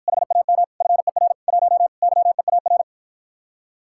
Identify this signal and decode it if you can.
Morse code, heard as a keyed tone, about 53 wpm, 700 Hz. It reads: HAM FER 1 YEAR